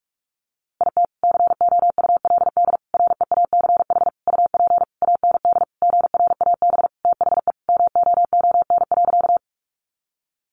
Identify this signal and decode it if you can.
Morse code, heard as a keyed tone, about 44 wpm, 700 Hz. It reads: IT COULD REACH UP AND GRAB THE MOON.
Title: A Sound Of Thunder 45wpm